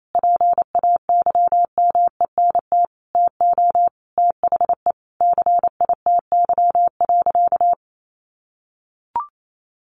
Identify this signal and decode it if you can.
Morse code, heard as a keyed tone, about 28 wpm, 700 Hz. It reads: PAYMENT TO THE CITY.
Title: Sentences from Top 1400 Words 28wpm